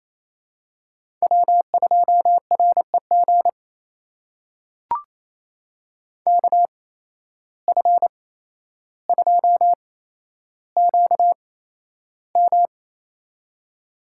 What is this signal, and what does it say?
Morse code, heard as a keyed tone, about 27 wpm, spaced as for 12 wpm, 700 Hz. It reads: W2REG KF2QM
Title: USA Call Signs - 5 Characters - 8x Character Spacing - 28wpm